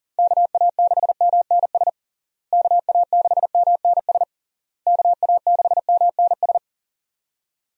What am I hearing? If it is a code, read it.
Morse code, heard as a keyed tone, about 40 wpm, 700 Hz. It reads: KA6MNS KA6MNS KA6MNS
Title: USA Call Signs - 6 Characters - Repeated 3x - 40wpm